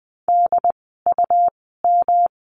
Morse code, heard as a keyed tone, about 20 wpm, spaced as for 14 wpm, 700 Hz. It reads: DUM
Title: 153 Lesson 25f - Letter p - Words 20wpm